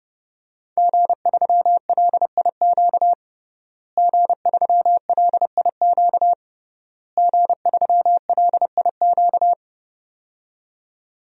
Morse code, heard as a keyed tone, about 30 wpm, 700 Hz. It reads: G3LIQ G3LIQ G3LIQ
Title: DXCC-100 Call Signs - 5 Characters - Repeated 3x - 30wpm